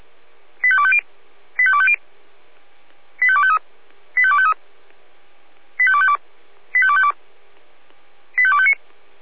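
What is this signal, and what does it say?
Five-tone selective call (ZVEI), heard as quick runs of five tones, tones of about 70 ms. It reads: 85250 85250 85253 85253 85252 85252 85259
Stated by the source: ZVEI five-tone call received by SDR.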